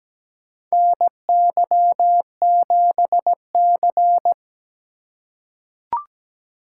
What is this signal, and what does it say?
Morse code, heard as a keyed tone, about 17 wpm, 700 Hz. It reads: NY7C
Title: USA Call Signs - 4 Characters - 4x Character Spacing - 17wpm